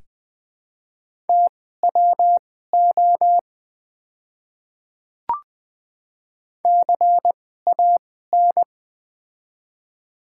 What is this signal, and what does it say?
Morse code, heard as a keyed tone, about 20 wpm, spaced as for 15 wpm, 690 Hz. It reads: TWO CAN
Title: 2x Character Spacing - Top 200 Words - 20wpm